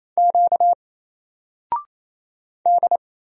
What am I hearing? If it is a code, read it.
Morse code, heard as a keyed tone, about 28 wpm, 700 Hz. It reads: Q D
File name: Single-Letter-Number-28wpm